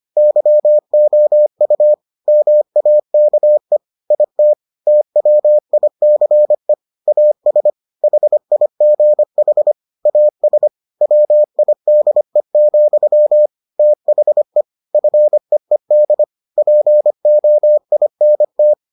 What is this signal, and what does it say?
Morse code, heard as a keyed tone, about 25 wpm, 600 Hz. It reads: YOU MAKE IT TWICE AS HIGH AS WIDE, THE FEED POINT